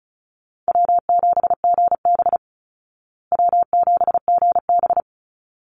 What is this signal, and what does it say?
Morse code, heard as a keyed tone, about 34 wpm, 700 Hz. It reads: W7GB W7GB
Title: Easy Call Signs Repeated 2x - 35wpm